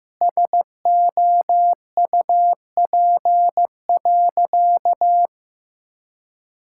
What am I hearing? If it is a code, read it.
Morse code, heard as a keyed tone, about 15 wpm, 700 Hz. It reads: SOUP.
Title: Sentences from Top 1000 Words - Review Part 6 - 15wpm